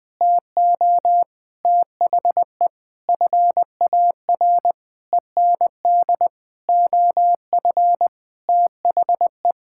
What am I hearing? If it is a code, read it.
Morse code, heard as a keyed tone, about 19 wpm, 700 Hz. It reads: TO THE FAR END OF THE
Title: Sentences with 5-Word Phrases - Spoken Beginning - 20wpm